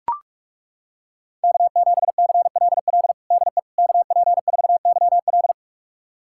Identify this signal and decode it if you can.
Morse code, heard as a keyed tone, about 42 wpm, 700 Hz. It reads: K7CLL DE KP4YL
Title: USA Callsign DE Callsign - 5 Characters - 45wpm